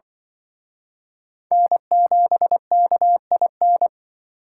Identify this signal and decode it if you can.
Morse code, heard as a keyed tone, about 24 wpm, 700 Hz. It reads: N7KIN
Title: Speed Racing - USA 1x3 Call Signs 20wpm